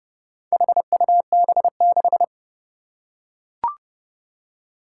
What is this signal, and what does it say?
Morse code, heard as a keyed tone, about 30 wpm, 700 Hz. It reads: HUB6
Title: Four Challenging Letter-Number Combinations 30wpm